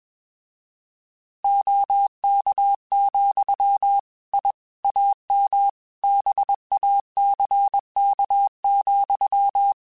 Morse code, heard as a keyed tone, about 21 wpm, 800 Hz. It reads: OK, I AM BACK,
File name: QSOs / QSOs-mp3s / kc7sac@20